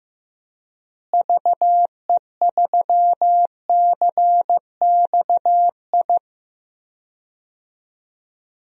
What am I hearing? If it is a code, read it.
Morse code, heard as a keyed tone, about 15 wpm, 700 Hz. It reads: VE3CXI
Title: DXCC-100 Call Signs - 6 Characters - Repeated 3x - 15wpm